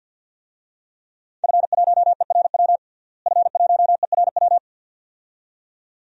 Morse code, heard as a keyed tone, about 52 wpm, 700 Hz. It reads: F1ERW F1ERW
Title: DXCC-100 Call Signs - 5 Characters - Repeated 2x - 50wpm